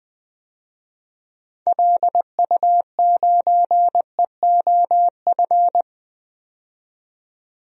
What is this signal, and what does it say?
Morse code, heard as a keyed tone, about 20 wpm, 700 Hz. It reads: LU9EOF
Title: DXCC-100 Call Signs - 6 Characters - 20wpm